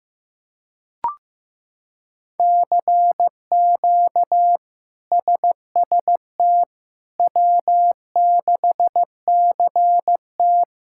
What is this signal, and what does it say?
Morse code, heard as a keyed tone, about 15 wpm, 700 Hz. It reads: CQ SST W6CT
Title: K1USN SST QSO Practice - 15wpm